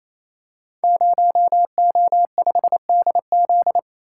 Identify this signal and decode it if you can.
Morse code, heard as a keyed tone, about 28 wpm, 700 Hz. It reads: 0O5DZ
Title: Code Groups - 5 Letter:Numbers - 28wpm